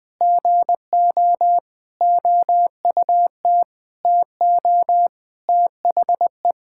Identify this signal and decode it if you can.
Morse code, heard as a keyed tone, about 20 wpm, 700 Hz. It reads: GO OUT TO THE
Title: Sentences with 8-Word Phrases - 20wpm